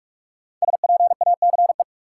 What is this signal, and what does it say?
Morse code, heard as a keyed tone, about 46 wpm, 700 Hz. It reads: SPACE
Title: Sentences from Top 1000 Words - Review Part 5 - New Word Drill - 45wpm